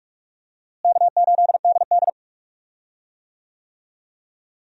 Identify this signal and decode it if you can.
Morse code, heard as a keyed tone, about 46 wpm, 700 Hz. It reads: K8DD
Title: USA Call Signs - 4 Characters - Repeated 3x - 45wpm